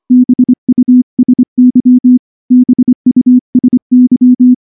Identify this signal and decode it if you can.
Morse code, heard as a keyed tone, about 25 wpm, 250 Hz. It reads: BUSY BUSY
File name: morse-training-25WPM